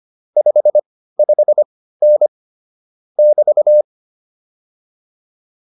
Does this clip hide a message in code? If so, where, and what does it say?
Yes: Morse code from 0.4 to 3.8 s, heard as a keyed tone, about 25 wpm, spaced as for 15 wpm, 600 Hz. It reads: 55N <BT>